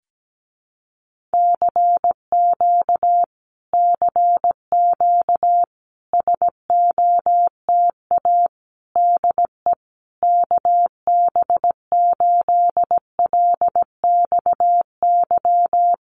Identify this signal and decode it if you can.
Morse code, heard as a keyed tone, about 17 wpm, 700 Hz. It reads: CQ CQ SOTA DE KB8LXY
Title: Common QSO Phrases Encore 17wpm